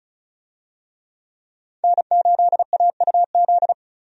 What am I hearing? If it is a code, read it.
Morse code, heard as a keyed tone, about 36 wpm, 700 Hz. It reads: N8AUZ
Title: USA Call Signs - 5 Characters - 35wpm